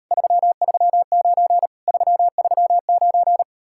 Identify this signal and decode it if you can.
Morse code, heard as a keyed tone, about 38 wpm, 700 Hz. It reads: 339 339